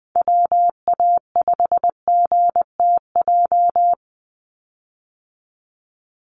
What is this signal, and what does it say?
Morse code, heard as a keyed tone, about 20 wpm, 700 Hz. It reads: WA5GTJ
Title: USA Call Signs 20wpm